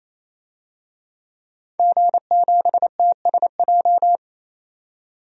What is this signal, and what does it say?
Morse code, heard as a keyed tone, about 27 wpm, 700 Hz. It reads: G7TSJ
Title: DXCC-100 Call Signs - 5 Characters - 28wpm